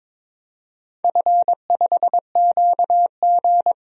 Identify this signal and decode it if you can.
Morse code, heard as a keyed tone, about 22 wpm, 700 Hz. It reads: F5QG
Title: DXCC-100 Call Signs - 4 Characters - Repeated 2x - 22wpm